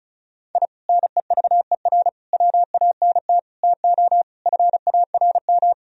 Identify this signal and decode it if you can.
Morse code, heard as a keyed tone, about 36 wpm, 700 Hz. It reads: I NEVER WANT TO FARM
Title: Sentences from Top 1000 Words - Review Part 2 - 35wpm